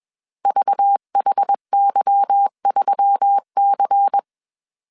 Morse code, heard as a keyed tone, about 21 wpm, 800 Hz. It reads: VHY3C
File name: CW / DET / 1 9 1500C S C 20wpm 2310